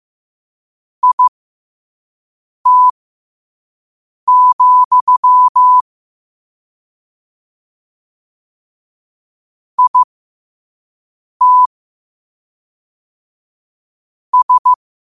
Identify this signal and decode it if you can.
Morse code, heard as a keyed tone, about 15 wpm, spaced as for 5 wpm, 1000 Hz. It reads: IT, IT S
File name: qotdfiles / 5wpm